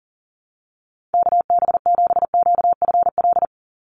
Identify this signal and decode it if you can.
Morse code, heard as a keyed tone, about 40 wpm, 700 Hz. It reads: KB7QFL
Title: USA Call Signs 40wpm